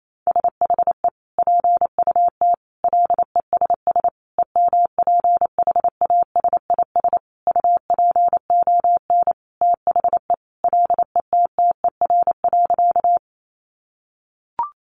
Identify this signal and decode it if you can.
Morse code, heard as a keyed tone, about 28 wpm, 700 Hz. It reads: SHE PUT LESS EMPHASIS UPON THE LETTER.
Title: Sentences from Top 1400 Words 28wpm